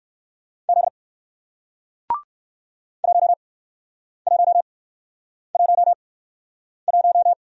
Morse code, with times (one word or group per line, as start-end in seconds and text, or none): 0.7-0.9 Z
3.0-7.3 1111